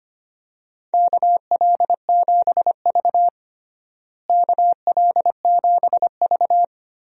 Morse code, heard as a keyed tone, about 25 wpm, 700 Hz. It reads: KL7V KL7V
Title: DXCC-100 Call Signs - 4 Characters - Repeated 2x - 25wpm